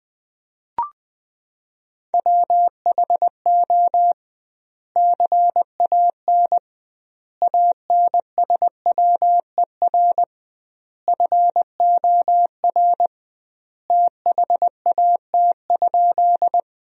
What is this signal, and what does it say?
Morse code, heard as a keyed tone, about 20 wpm, 700 Hz. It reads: WHO CAN ANSWER FOR THAT?
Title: 2x Word-Spacing - Sentences from Top 300 Words - 20wpm